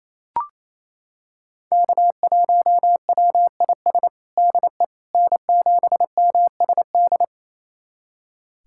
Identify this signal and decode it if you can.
Morse code, heard as a keyed tone, about 28 wpm, 700 Hz. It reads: K1WIS DE N7MSD
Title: USA Callsign DE Callsign 28wpm